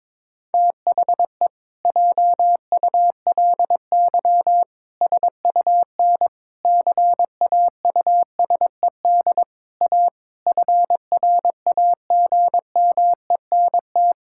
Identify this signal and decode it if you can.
Morse code, heard as a keyed tone, about 22 wpm, 700 Hz. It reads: THE JULY SUN CAUSED A FRAGMENT
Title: Pangram Challenge v2 - 22wpm